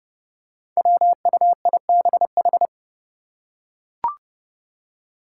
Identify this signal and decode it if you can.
Morse code, heard as a keyed tone, about 32 wpm, 700 Hz. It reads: WUIBH
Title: Five Challenging Letter-Number Combinations 30wpm